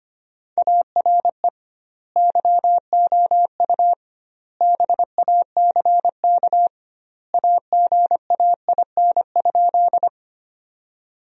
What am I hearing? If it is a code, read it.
Morse code, heard as a keyed tone, about 25 wpm, 700 Hz. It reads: ARE YOU BACK AGAIN?
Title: 2x Word-Spacing - Sentences from Top 200 Words - 25wpm